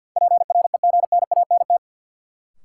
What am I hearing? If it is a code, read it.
Morse code, heard as a keyed tone, about 53 wpm, 700 Hz. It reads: PREGNANT
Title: Sentences from Top 1000 Words - Review Part 6 - New Word Drill - 50wpm